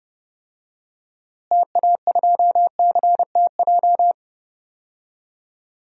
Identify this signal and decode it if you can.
Morse code, heard as a keyed tone, about 30 wpm, 700 Hz. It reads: TA2CTJ
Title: DXCC-100 Call Signs - 6 Characters - Repeated 2x - 30wpm